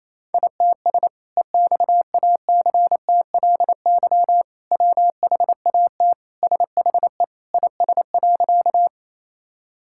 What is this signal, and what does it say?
Morse code, heard as a keyed tone, about 28 wpm, 700 Hz. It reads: ITS EXACTLY WHAT SHE IS.
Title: Sentences from Top 1000 Words - Review Part 6 - 28wpm